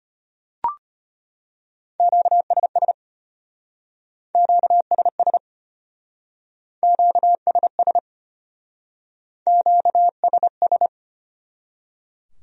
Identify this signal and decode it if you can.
Morse code, heard as a keyed tone, about 33 wpm, 700 Hz. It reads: QSS QSS QSS QSS